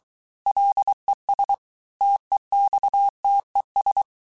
Morse code, heard as a keyed tone, about 23 wpm, 800 Hz. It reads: LES TEXTES
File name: cours-du-mercredi / audios / cours-855-texte-chiffre-vitesse-22-mpm